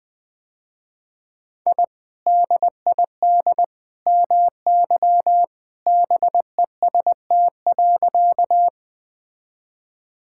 Morse code, heard as a keyed tone, about 21 wpm, 700 Hz. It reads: I DID MY BEST.
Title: Sentences with 2-Word Phrases - Spoken Beginning - 20wpm